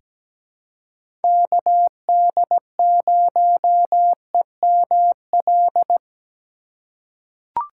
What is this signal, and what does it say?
Morse code, heard as a keyed tone, about 17 wpm, 700 Hz. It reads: KD0EML
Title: USA Call Signs - 6 Characters - 4x Character Spacing - 17wpm